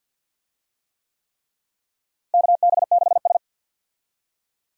Morse code, heard as a keyed tone, about 52 wpm, 700 Hz. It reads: KB6S
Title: USA Call Signs - 4 Characters - Repeated 3x - 50wpm